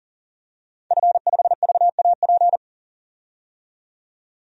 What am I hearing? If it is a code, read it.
Morse code, heard as a keyed tone, about 40 wpm, 700 Hz. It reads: F5VAP
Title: DXCC-100 Call Signs - 5 Characters - 40wpm